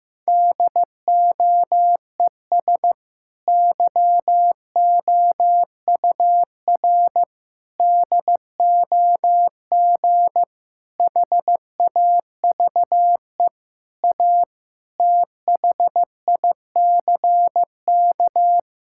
Morse code, heard as a keyed tone, about 15 wpm, 700 Hz. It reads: DOES YOUR DOG HAVE A THICK
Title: Sentences from Top 1600 Words 15wpm